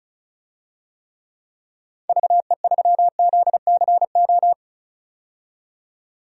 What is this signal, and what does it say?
Morse code, heard as a keyed tone, about 34 wpm, 700 Hz. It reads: VE3ZCO
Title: DXCC-100 Call Signs - 6 Characters - 35wpm